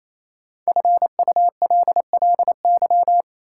Morse code, heard as a keyed tone, about 27 wpm, 700 Hz. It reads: FULLY
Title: Sentences from Top 1300 Words - New Word Drill Alt - 28wpm